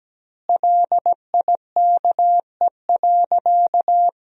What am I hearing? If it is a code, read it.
Morse code, heard as a keyed tone, about 17 wpm, 700 Hz. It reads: LIKE.
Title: 2x Word-Spacing - Sentences from Top 500 Words - 17wpm